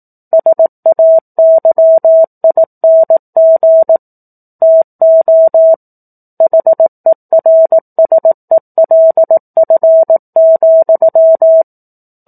Morse code, heard as a keyed tone, about 18 wpm, 650 Hz. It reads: SAYING TO HERSELF,
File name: 18 WPM